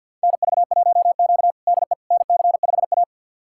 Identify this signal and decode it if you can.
Morse code, heard as a keyed tone, about 52 wpm, 700 Hz. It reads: NV1Q DE NC5A
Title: USA Callsign DE Callsign - 4 Characters - 50wpm